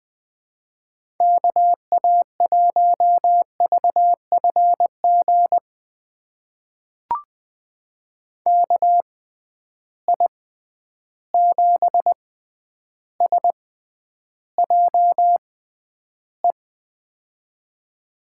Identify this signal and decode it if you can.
Morse code, heard as a keyed tone, about 20 wpm, spaced as for 10 wpm, 700 Hz. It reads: KA1VFG KI7SJE
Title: USA Call Signs - 6x Character Spacing - 20wpm